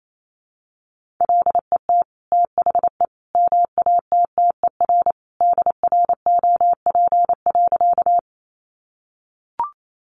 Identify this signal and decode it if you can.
Morse code, heard as a keyed tone, about 28 wpm, 700 Hz. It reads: LET THE MATTER DROP.
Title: Sentences from Top 1400 Words 28wpm